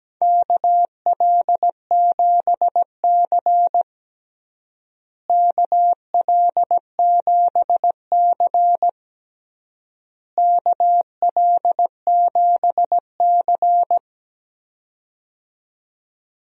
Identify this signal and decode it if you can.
Morse code, heard as a keyed tone, about 17 wpm, 700 Hz. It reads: KL7C KL7C KL7C
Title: DXCC-100 Call Signs - 4 Characters - Repeated 3x - 17wpm